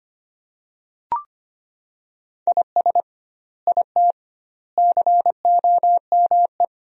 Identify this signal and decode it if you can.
Morse code, heard as a keyed tone, about 25 wpm, 700 Hz. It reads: IS IT COME
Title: 2x Word-Spacing - Sentences from Top 500 Words - 25wpm